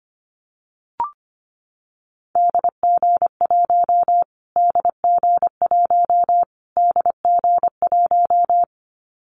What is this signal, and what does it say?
Morse code, heard as a keyed tone, about 25 wpm, 700 Hz. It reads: DG1 DG1 DG1
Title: Callsign Prefixes for Top 25 DXCC - Repeated 3x - 25wpm